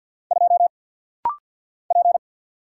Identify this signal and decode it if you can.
Morse code, heard as a keyed tone, about 50 wpm, 700 Hz. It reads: 2 P
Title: Single Letter-Number - Rapid-Fire 50wpm